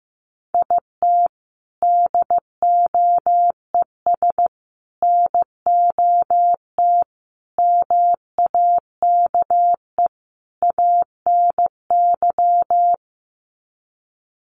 Morse code, heard as a keyed tone, about 15 wpm, 700 Hz. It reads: IT DOES NOT MAKE ANY
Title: Sets of 5 Words Encore2 15wpm